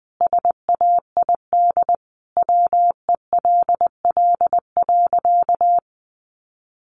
Morse code, heard as a keyed tone, about 20 wpm, 700 Hz. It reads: SAID WELL.
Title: Sentences from Top 1100 Words 20wpm